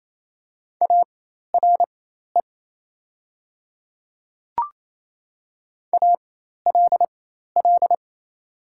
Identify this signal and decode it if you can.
Morse code, heard as a keyed tone, about 27 wpm, spaced as for 13 wpm, 700 Hz. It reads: ARE ALL
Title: 4x Character Spacing - Top 100 Words - 28wpm